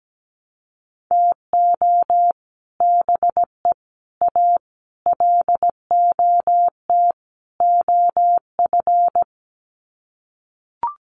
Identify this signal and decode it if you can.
Morse code, heard as a keyed tone, about 17 wpm, 700 Hz. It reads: TO BE A LOT OF
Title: Sets of 5 Words Encore 17wpm